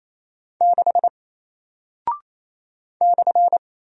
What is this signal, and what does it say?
Morse code, heard as a keyed tone, about 28 wpm, 700 Hz. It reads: 6 /
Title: Single Character (Letter:Number:Punctuation:Prosign) Alt 28wpm